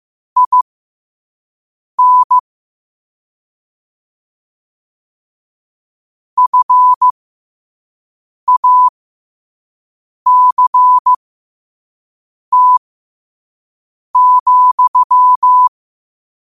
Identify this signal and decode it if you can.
Morse code, heard as a keyed tone, about 15 wpm, spaced as for 5 wpm, 1000 Hz. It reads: IN FACT,
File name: qotdfiles / 5wpm